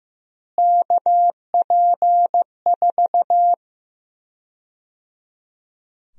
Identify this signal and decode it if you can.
Morse code, heard as a keyed tone, about 15 wpm, 700 Hz. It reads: KP4
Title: USA Call Sign Prefixes - Repeated 2x - 15wpm